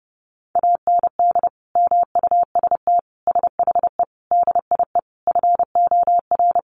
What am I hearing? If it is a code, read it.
Morse code, heard as a keyed tone, about 30 wpm, 700 Hz. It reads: AND MUST SHE DIE FOR
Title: Sentences from Top 1300 Words 30wpm